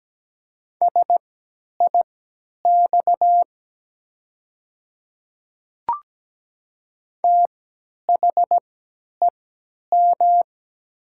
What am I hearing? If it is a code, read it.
Morse code, heard as a keyed tone, about 17 wpm, spaced as for 10 wpm, 700 Hz. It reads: SIX THEM
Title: 3x Character Spacing - Top 500 Words - 17wpm